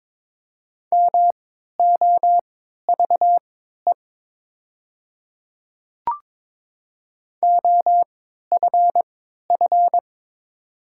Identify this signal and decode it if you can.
Morse code, heard as a keyed tone, about 22 wpm, spaced as for 12 wpm, 700 Hz. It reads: MOVE OFF
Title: 3x Character Spacing - Top 200 Words - 22wpm